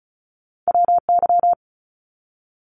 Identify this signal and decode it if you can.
Morse code, heard as a keyed tone, about 36 wpm, 700 Hz. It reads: WY
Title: ARRL Sections 35wpm